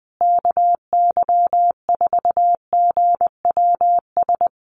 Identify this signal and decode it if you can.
Morse code, heard as a keyed tone, about 20 wpm, 700 Hz. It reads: KY4GWS
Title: USA Call Signs Repeated 3x 20wpm